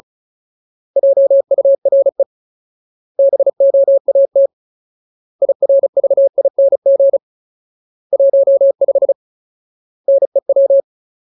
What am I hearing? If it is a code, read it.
Morse code, heard as a keyed tone, about 34 wpm, 550 Hz. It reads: JURE BOAT IRVING 15 NEW
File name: Ultimate Word List / 35WPM Ultimate